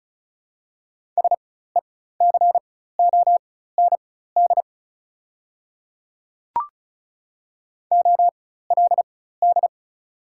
Morse code, heard as a keyed tone, about 36 wpm, spaced as for 16 wpm, 700 Hz. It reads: SECOND OLD
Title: 4x Character Spacing - Top 500 Words - 35wpm